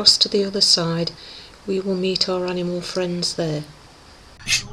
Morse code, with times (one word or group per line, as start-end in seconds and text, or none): none